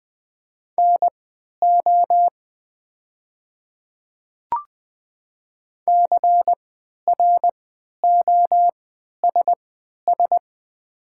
Morse code, heard as a keyed tone, about 20 wpm, spaced as for 11 wpm, 700 Hz. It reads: NO CROSS
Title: 3x Character Spacing - Top 300 Words - 20wpm